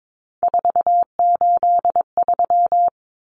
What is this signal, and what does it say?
Morse code, heard as a keyed tone, about 22 wpm, 700 Hz. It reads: 483
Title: Numbers - 3-Digit 22wpm